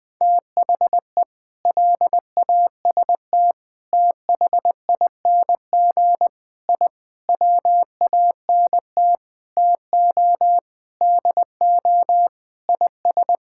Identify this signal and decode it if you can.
Morse code, heard as a keyed tone, about 20 wpm, 700 Hz. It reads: THE LAST THING I WANT TO DO IS
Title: Sentences with 7-Word Phrases - 20wpm